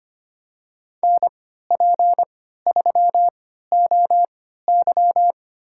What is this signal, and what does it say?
Morse code, heard as a keyed tone, about 25 wpm, spaced as for 14 wpm, 700 Hz. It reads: NP3OY
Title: USA Call Signs - 5 Characters - 3x Character Spacing - 25wpm